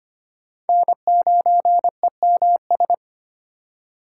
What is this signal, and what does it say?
Morse code, heard as a keyed tone, about 25 wpm, 700 Hz. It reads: N9EMS
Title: USA Call Signs - 4x Character Spacing - 25wpm